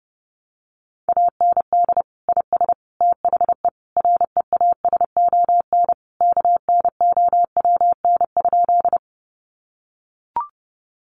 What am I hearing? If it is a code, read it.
Morse code, heard as a keyed tone, about 30 wpm, 700 Hz. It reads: AND IS THE REASON KNOWN?
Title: Sentences from Top 700 Words 30wpm